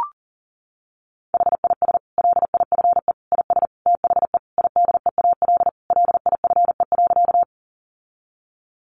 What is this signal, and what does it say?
Morse code, heard as a keyed tone, about 40 wpm, 700 Hz. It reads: HIS LIFE IS THE IDEAL LIFE.
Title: Sentences from Top 1400 Words 40wpm